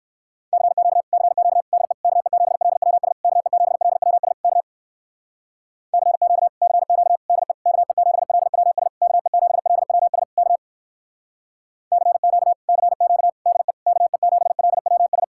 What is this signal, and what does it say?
Morse code, heard as a keyed tone, about 58 wpm, 700 Hz. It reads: CQ CQ DE KE7LWS KE7LWS K CQ CQ DE KE7LWS KE7LWS K CQ CQ DE KE7LWS